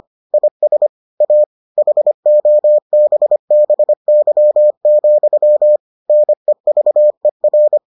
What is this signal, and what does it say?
Morse code, heard as a keyed tone, about 25 wpm, 600 Hz. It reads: IS A HOBBY, NEVER